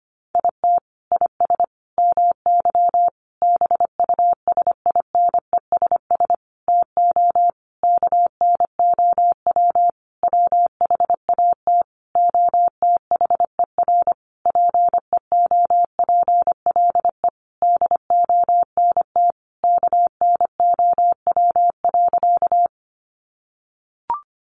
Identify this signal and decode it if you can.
Morse code, heard as a keyed tone, about 25 wpm, 700 Hz. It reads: IT IS MY BUSINESS TO KNOW WHAT OTHER PEOPLE DONT KNOW.
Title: Sentences from Top 600 Words 25wpm